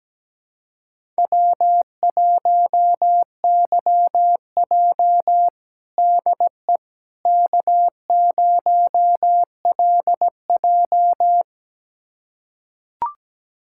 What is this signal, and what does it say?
Morse code, heard as a keyed tone, about 17 wpm, 700 Hz. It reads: W1YJ DE K0LJ
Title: USA Callsign DE Callsign - 4 Characters - 17wpm